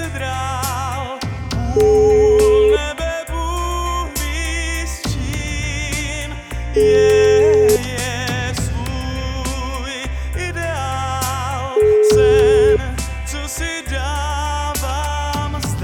Ringback tone (Europe, 425 Hz), with noise.